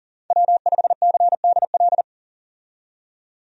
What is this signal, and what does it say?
Morse code, heard as a keyed tone, about 37 wpm, 700 Hz. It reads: W5CDL
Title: USA Call Signs - 5 Characters - 40wpm